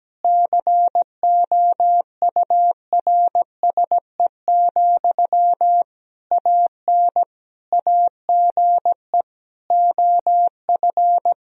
Morse code, heard as a keyed tone, about 17 wpm, 700 Hz. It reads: COURSE, AN AGE OF
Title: Sentences from Top 1000 Words - Review Part 3 - 17wpm